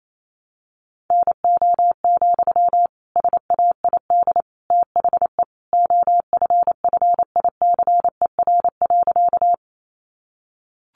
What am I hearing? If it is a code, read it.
Morse code, heard as a keyed tone, about 28 wpm, 700 Hz. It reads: NO, SAID THE OFFICER.
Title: Sentences from Top 1200 Words 28wpm